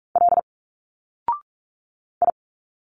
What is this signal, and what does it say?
Morse code, heard as a keyed tone, about 46 wpm, 700 Hz. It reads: L I
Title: Single Letters 45wpm